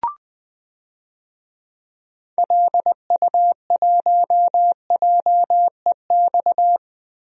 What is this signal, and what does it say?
Morse code, heard as a keyed tone, about 20 wpm, 700 Hz. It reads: LU1JEX
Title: DXCC-100 Call Signs - 6 Characters - 20wpm